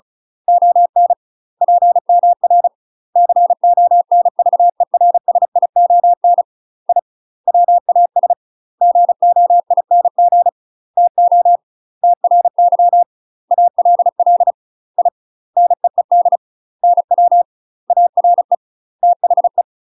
Morse code, heard as a keyed tone, about 34 wpm, 700 Hz. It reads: ON PMR CONVERSION I WAS GOING TO TRY ALL I NEED NW ARE THE
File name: UK MASTER QSOs TRAINING / 35WPM / UK MASTER QSOs 0006